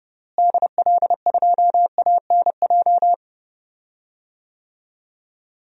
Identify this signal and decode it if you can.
Morse code, heard as a keyed tone, about 30 wpm, 700 Hz. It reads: DL2ANJ
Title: DXCC-100 Call Signs - 6 Characters - Repeated 3x - 30wpm